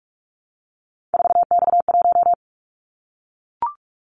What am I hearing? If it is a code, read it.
Morse code, heard as a keyed tone, about 44 wpm, 700 Hz. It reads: 4X1
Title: Callsign Prefixes for Top 50 DXCC - 45wpm